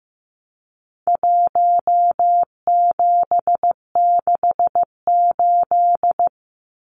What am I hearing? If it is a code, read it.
Morse code, heard as a keyed tone, about 15 wpm, 700 Hz. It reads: 1768
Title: Numbers - Contest Style - 4-Digit 15wpm